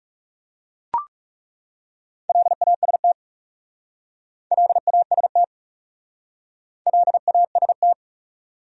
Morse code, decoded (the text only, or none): LAST LAST LAST